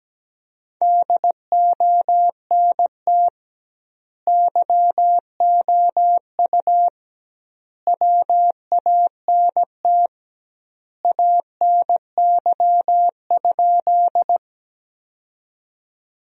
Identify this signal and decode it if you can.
Morse code, heard as a keyed tone, about 17 wpm, 700 Hz. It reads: DONT YOU WANT ANY?
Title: 2x Word-Spacing - Sentences from Top 300 Words - 17wpm